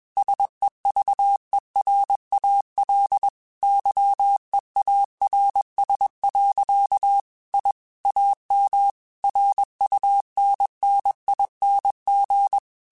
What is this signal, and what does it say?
Morse code, heard as a keyed tone, about 21 wpm, 800 Hz. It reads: SEVERAL YEARS. I AM RUNNING
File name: QSOs / QSOs-mp3s / WA7QOX 2@20